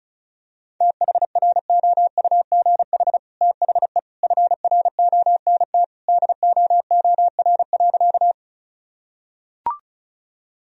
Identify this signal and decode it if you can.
Morse code, heard as a keyed tone, about 34 wpm, 700 Hz. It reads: THROUGH THE FRONT DOOR.
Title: Sentences from Top 1000 Words - Review Part 2 - 35wpm